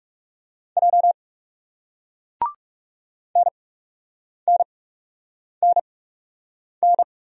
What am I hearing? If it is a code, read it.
Morse code, heard as a keyed tone, about 41 wpm, spaced as for 8 wpm, 700 Hz. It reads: J NNNN